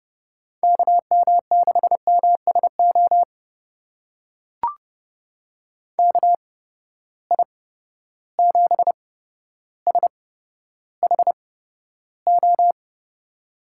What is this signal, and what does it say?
Morse code, heard as a keyed tone, about 31 wpm, spaced as for 13 wpm, 700 Hz. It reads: KM6MSO KI7SHO
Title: USA Call Signs - 6 Characters - 8x Character Spacing - 30wpm